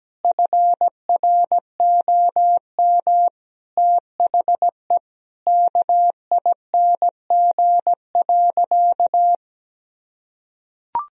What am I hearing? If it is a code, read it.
Morse code, heard as a keyed tone, about 17 wpm, 700 Hz. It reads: FROM THE KING.
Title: Sentences from Top 1000 Words - Review Part 2 - 17wpm